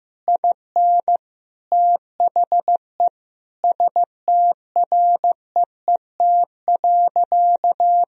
Morse code, heard as a keyed tone, about 15 wpm, 700 Hz. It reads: IN THE STREET.
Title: Sentences from Top 1000 Words - Review Part 3 - 15wpm